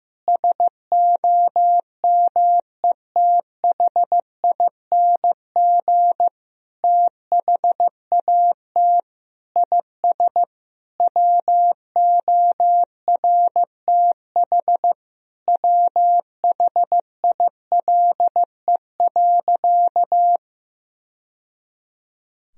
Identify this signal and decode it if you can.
Morse code, heard as a keyed tone, about 16 wpm, 700 Hz. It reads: SOMETHING THAT IS WORTH WHILE.
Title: Sentences from Top 1000 Words - Review Part 1 - 15wpm